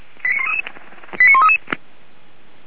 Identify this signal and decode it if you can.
Five-tone selective call (ZVEI), heard as quick runs of five tones, tones of about 80 ms. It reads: 89133